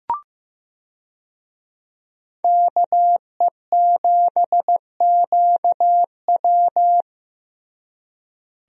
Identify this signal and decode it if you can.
Morse code, heard as a keyed tone, about 15 wpm, 700 Hz. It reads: KE7QW
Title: USA Call Signs - 5 Characters - Repeated 2x - 15wpm